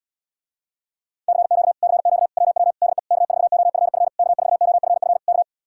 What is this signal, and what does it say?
Morse code, heard as a keyed tone, about 77 wpm, 700 Hz. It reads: CQ CQ WW DE K4OFF K4OFF K